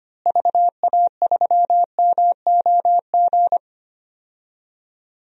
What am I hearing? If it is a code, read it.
Morse code, heard as a keyed tone, about 25 wpm, 700 Hz. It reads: VA3MOG
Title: DXCC-100 Call Signs - 6 Characters - Repeated 2x - 25wpm